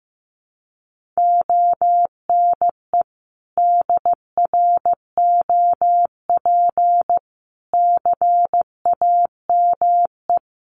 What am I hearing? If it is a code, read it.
Morse code, heard as a keyed tone, about 15 wpm, 700 Hz. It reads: ONE DROP CAME
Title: Sentences from Top 1400 Words 15wpm